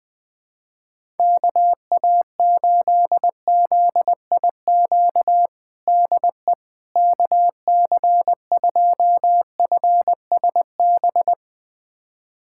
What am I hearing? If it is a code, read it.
Morse code, heard as a keyed tone, about 20 wpm, 700 Hz. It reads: KA8ZIQ DE KC2FSB
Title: USA Callsign DE Callsign 20wpm